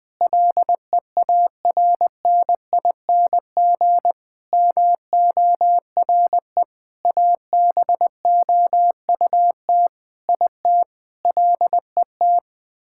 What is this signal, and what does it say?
Morse code, heard as a keyed tone, about 21 wpm, 700 Hz. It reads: LEARNING MORE ABOUT IT LET
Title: Sentences with 8-Word Phrases - Spoken Beginning - 20wpm